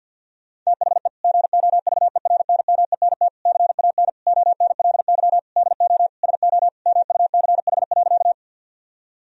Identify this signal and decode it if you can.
Morse code, heard as a keyed tone, about 50 wpm, 700 Hz. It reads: THE GOVERNMENT CAN ONLY DO SO MUCH.
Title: Sentences from Top 1000 Words - Review Part 3 - 50wpm